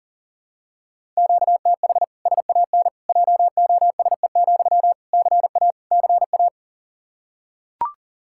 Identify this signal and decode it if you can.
Morse code, heard as a keyed tone, about 40 wpm, 700 Hz. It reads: QTH SAN JOSE, CA CA
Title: Practice QSO - Exchange 1 - Beginning - 40wpm